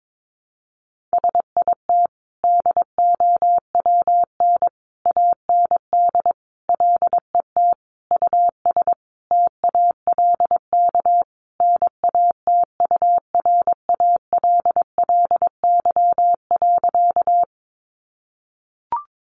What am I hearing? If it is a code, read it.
Morse code, heard as a keyed tone, about 22 wpm, 700 Hz. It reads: SIT DOWN AND LET US TALK NATURALLY.
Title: Sentences from Top 1300 Words 22wpm